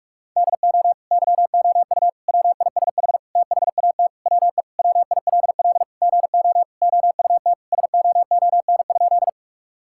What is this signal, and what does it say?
Morse code, heard as a keyed tone, about 41 wpm, 700 Hz. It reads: DO YOU WISH THAT WE WILL GO OUT SOON?
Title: Sentences from Top 1600 Words 45wpm